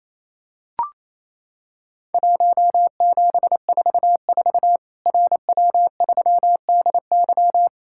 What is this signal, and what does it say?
Morse code, heard as a keyed tone, about 28 wpm, 700 Hz. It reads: 1744 RW3DY
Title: Worked All Europe DX Contest - QTC Exchange - 28wpm